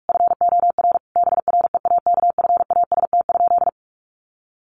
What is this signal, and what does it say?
Morse code, heard as a keyed tone, about 44 wpm, 700 Hz. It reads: FOR BREAKFAST?
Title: Sentences from Top 1500 Words 45wpm